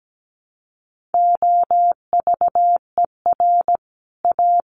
Morse code, heard as a keyed tone, about 17 wpm, 700 Hz. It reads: OVER A
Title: Sets of 5 Words Encore2 17wpm